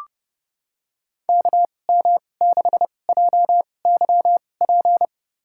Morse code, heard as a keyed tone, about 30 wpm, spaced as for 22 wpm, 700 Hz. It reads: KM6JYP
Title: USA Call Signs - 6 Characters - 2x Character Spacing - 30wpm